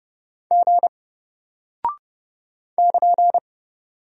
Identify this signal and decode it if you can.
Morse code, heard as a keyed tone, about 30 wpm, 700 Hz. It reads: G <KN>
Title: Single Character (Letter:Number:Punctuation:Prosign) Alt 30wpm